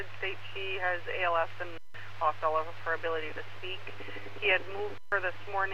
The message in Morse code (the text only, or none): none